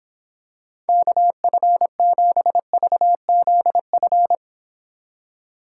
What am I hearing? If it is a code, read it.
Morse code, heard as a keyed tone, about 26 wpm, 700 Hz. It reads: KF7VZF